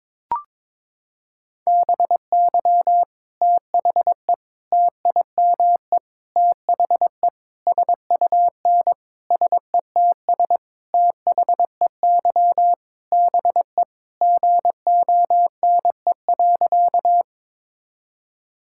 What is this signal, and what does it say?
Morse code, heard as a keyed tone, about 22 wpm, 700 Hz. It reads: BY THE TIME THE SUN SETS THEY BE GONE.
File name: practice / 22wpm / Sentences with 4-Word Phrases - 22wpm